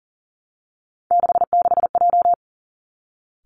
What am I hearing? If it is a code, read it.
Morse code, heard as a keyed tone, about 42 wpm, 690 Hz. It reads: 66J
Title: Three Challenging Letter-Number Combinations 40wpm